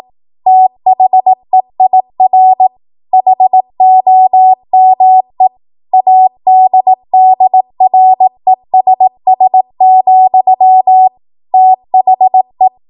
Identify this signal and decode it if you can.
Morse code, heard as a keyed tone, about 18 wpm, 750 Hz. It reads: THEIR HOME ADDRESS, THE
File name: Morse / 18WPM